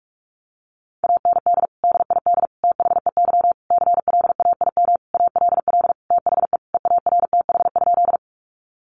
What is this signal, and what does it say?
Morse code, heard as a keyed tone, about 45 wpm, 700 Hz. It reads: AND DID THEY CLAIM ALL THE EARTH?
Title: Sentences from Top 1500 Words 45wpm